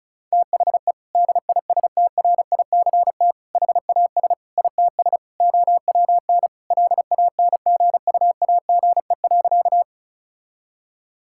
Morse code, heard as a keyed tone, about 36 wpm, 700 Hz. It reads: THE DISTRICT HAS ITS OWN LANGUAGE.
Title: Sentences from Top 1600 Words 35wpm